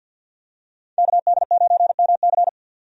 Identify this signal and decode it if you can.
Morse code, heard as a keyed tone, about 50 wpm, 700 Hz. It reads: KD9MC
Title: USA Call Signs - 5 Characters - 50wpm